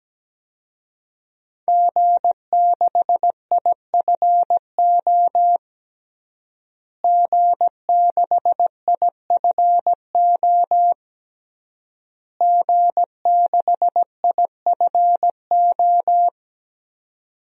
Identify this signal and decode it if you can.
Morse code, heard as a keyed tone, about 17 wpm, 700 Hz. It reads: G6IFO G6IFO G6IFO
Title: DXCC-100 Call Signs - 5 Characters - Repeated 3x - 17wpm